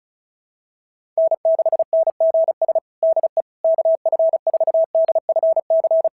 Morse code, heard as a keyed tone, about 36 wpm, 650 Hz. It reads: N6NGS DE KF4DFC
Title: SOTA QSO Practice 35wpm